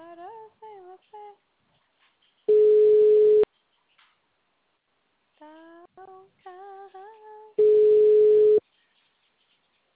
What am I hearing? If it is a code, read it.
Ringback tone (Europe, 425 Hz), with speech.
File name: out- -2006- - - .87155